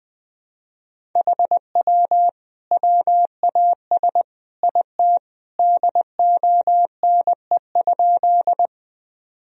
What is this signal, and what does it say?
Morse code, heard as a keyed tone, about 20 wpm, 700 Hz. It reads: HW WAS IT DONE?
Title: Sentences from Top 500 Words and Abbreviations 20wpm